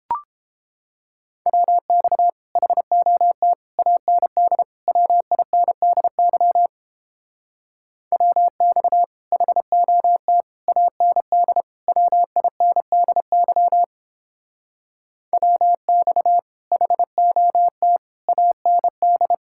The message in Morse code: WX HOT AND WINDY WX HOT AND WINDY WX HOT AND

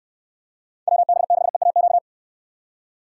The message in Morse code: WB6EAQ

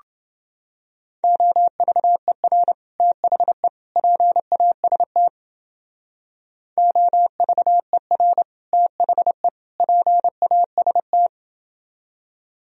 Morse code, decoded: OVER THE PAST OVER THE PAST

